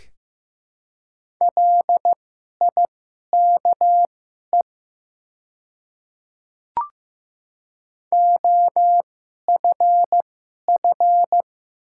Morse code, heard as a keyed tone, about 15 wpm, spaced as for 11 wpm, 700 Hz. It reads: LIKE OFF
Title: 2x Character Spacing - Top 200 Words - 15wpm